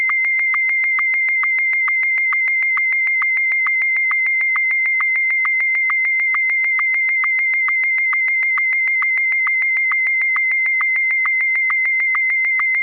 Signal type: SSTV (slow-scan television)